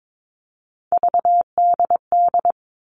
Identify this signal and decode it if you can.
Morse code, heard as a keyed tone, about 22 wpm, 700 Hz. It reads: VDD
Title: All Cut Numbers - 3-Digit 22wpm